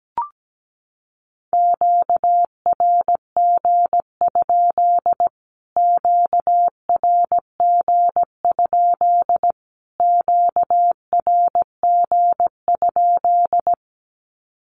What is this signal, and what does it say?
Morse code, heard as a keyed tone, about 17 wpm, 700 Hz. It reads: QRG? QRG? QRG?
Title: Q-Codes Repeated 3x 17wpm